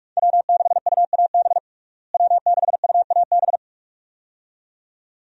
Morse code, heard as a keyed tone, about 42 wpm, 700 Hz. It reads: W6UAB W6UAB
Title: USA Call Signs - 5 Characters - Repeated 2x - 45wpm